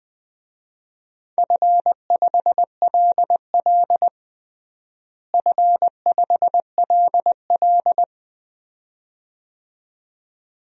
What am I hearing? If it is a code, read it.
Morse code, heard as a keyed tone, about 20 wpm, 700 Hz. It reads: F5LL F5LL
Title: DXCC-100 Call Signs - 4 Characters - Repeated 2x - 20wpm